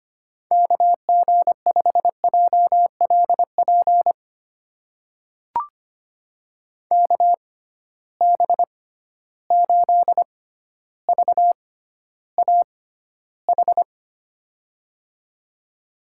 Morse code, heard as a keyed tone, about 25 wpm, spaced as for 13 wpm, 700 Hz. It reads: KG5JLP KB8VAH
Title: USA Call Signs - 6 Characters - 6x Character Spacing - 25wpm